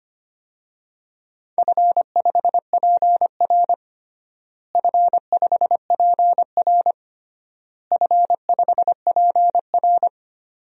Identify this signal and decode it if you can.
Morse code, heard as a keyed tone, about 25 wpm, 700 Hz. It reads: F5PR F5PR F5PR
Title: DXCC-100 Call Signs - 4 Characters - Repeated 3x - 25wpm